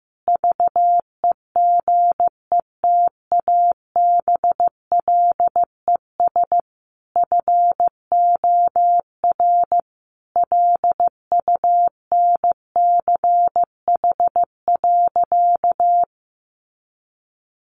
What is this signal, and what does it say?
Morse code, heard as a keyed tone, about 15 wpm, 700 Hz. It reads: VEGETABLES FOR LUNCH.
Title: Easy Phrases 15wpm